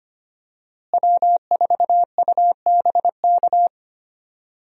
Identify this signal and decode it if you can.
Morse code, heard as a keyed tone, about 25 wpm, 700 Hz. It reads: W4UBK
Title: USA Call Signs - 5 Characters - Repeated 3x - 25wpm